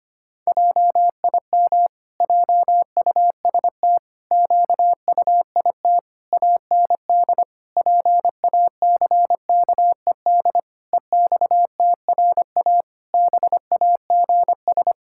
Morse code, heard as a keyed tone, about 25 wpm, 700 Hz. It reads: JIM JUST QUIT AND PACKED EXTRA BAGS
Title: Pangram Challenge v2 - 25wpm